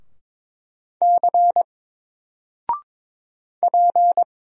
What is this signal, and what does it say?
Morse code, heard as a keyed tone, about 22 wpm, 700 Hz. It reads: C P
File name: Single-Letter-Number-22wpm